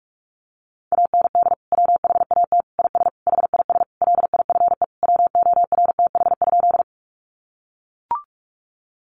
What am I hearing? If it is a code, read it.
Morse code, heard as a keyed tone, about 44 wpm, 700 Hz. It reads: AND WHAT IS HIS LIFE WORTH?
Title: Sentences from Top 800 Words 45wpm